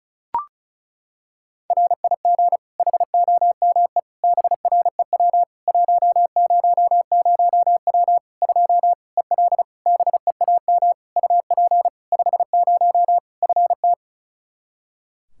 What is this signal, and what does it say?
Morse code, heard as a keyed tone, about 36 wpm, 700 Hz. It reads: RIG HOME BREW 100W 2 EL BEAM UP 50 FT
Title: Practice QSO - Exchange 2 - Info - 35wpm